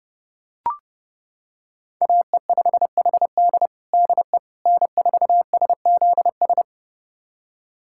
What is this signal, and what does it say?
Morse code, heard as a keyed tone, about 32 wpm, 700 Hz. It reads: AE5HD DE N4SZS
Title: USA Callsign DE Callsign - 5 Characters - 30wpm